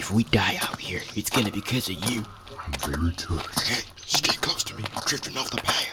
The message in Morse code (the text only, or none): none